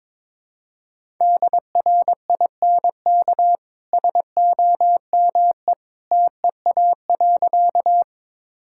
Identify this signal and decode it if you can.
Morse code, heard as a keyed tone, about 21 wpm, 700 Hz. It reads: DRINK SOME TEA.
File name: practice / 22wpm / Easy Sentences with Hints - 22wpm